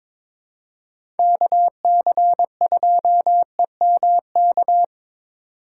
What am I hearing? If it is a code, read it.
Morse code, heard as a keyed tone, about 22 wpm, 700 Hz. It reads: KC2EMK
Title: USA Call Signs - 6 Characters - 8x Character Spacing - 22wpm